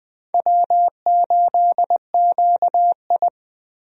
Morse code, heard as a keyed tone, about 20 wpm, 700 Hz. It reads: W8QI
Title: USA Call Signs - 4 Characters - Repeated 3x - 20wpm